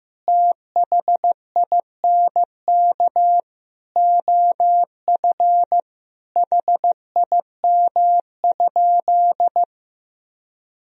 Morse code, heard as a keyed tone, about 15 wpm, 700 Hz. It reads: THINK OF HIM?
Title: Sentences from Top 200 Words and Abbreviations 15wpm